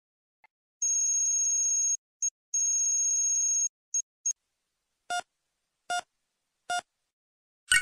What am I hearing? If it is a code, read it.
There is no code.